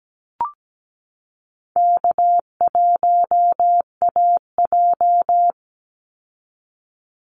Morse code, heard as a keyed tone, about 17 wpm, 700 Hz. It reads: K1AJ
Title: Call Signs Repeated 3x - 17wpm